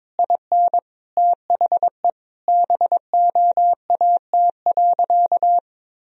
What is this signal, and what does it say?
Morse code, heard as a keyed tone, about 22 wpm, 700 Hz. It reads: IN THE BOAT.
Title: Sentences from Top 1000 Words - Review Part 3 - 22wpm